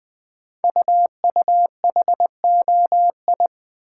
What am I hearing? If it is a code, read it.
Morse code, heard as a keyed tone, about 20 wpm, 700 Hz. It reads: UUHOI
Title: Code Groups - 5 Letters - 20wpm